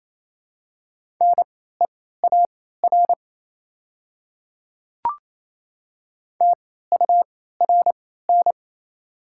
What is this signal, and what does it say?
Morse code, heard as a keyed tone, about 28 wpm, spaced as for 16 wpm, 700 Hz. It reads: NEAR TURN
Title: 3x Character Spacing - Top 200 Words - 28wpm